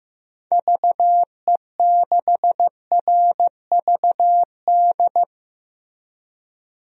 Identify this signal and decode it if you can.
Morse code, heard as a keyed tone, about 15 wpm, 700 Hz. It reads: VE6RVD
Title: DXCC-100 Call Signs - 6 Characters - Repeated 3x - 15wpm